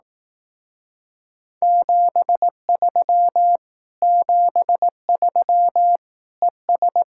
Morse code, heard as a keyed tone, about 18 wpm, 700 Hz. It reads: 73 73 ES
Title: Speed-Racing Common QSO Phrases Encore - 15wpm